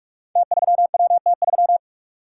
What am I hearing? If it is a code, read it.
Morse code, heard as a keyed tone, about 45 wpm, 700 Hz. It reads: T3WT3
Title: Code Groups - 5 Letter:Numbers - 45wpm